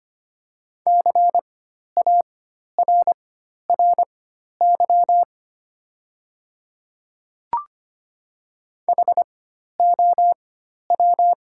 Morse code, heard as a keyed tone, about 25 wpm, spaced as for 12 wpm, 700 Hz. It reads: CARRY HOW
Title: 4x Character Spacing - Top 300 Words - 25wpm